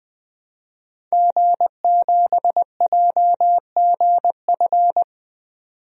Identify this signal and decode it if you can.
Morse code, heard as a keyed tone, about 20 wpm, 700 Hz. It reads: G7JGF
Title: DXCC-100 Call Signs - 5 Characters - 20wpm